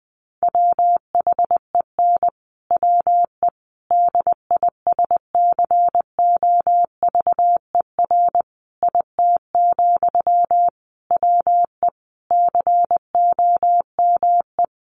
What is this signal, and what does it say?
Morse code, heard as a keyed tone, about 20 wpm, 700 Hz. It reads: WHEN WE DISCOVER IT, WE COME
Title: Sentences from Top 1100 Words 20wpm